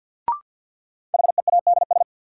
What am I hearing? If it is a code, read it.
Morse code, heard as a keyed tone, about 49 wpm, 700 Hz. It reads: HEADS
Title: Sentences from Top 1000 Words - Review Part 3 - New Word Drill - 50wpm